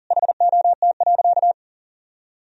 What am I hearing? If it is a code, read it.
Morse code, heard as a keyed tone, about 40 wpm, 700 Hz. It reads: HOT.
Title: Sentences from Top 1000 Words - Review Part 3 - 40wpm